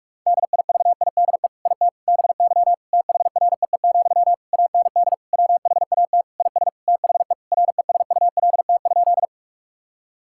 Morse code, heard as a keyed tone, about 46 wpm, 700 Hz. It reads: DIVIDE IT BY THREE, AND WHAT IS THE RESULT?
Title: Sentences from Top 1600 Words 45wpm